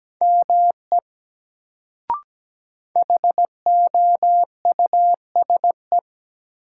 Morse code, heard as a keyed tone, about 17 wpm, 700 Hz. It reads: ME HOUSE
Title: Sentences from Top 200 Words - New Word Drill - 17wpm